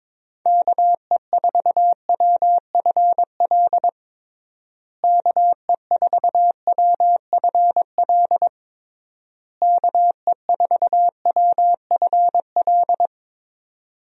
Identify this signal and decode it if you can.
Morse code, heard as a keyed tone, about 22 wpm, 700 Hz. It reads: KE4WFL KE4WFL KE4WFL
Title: USA Call Signs - 6 Characters - Repeated 3x - 22wpm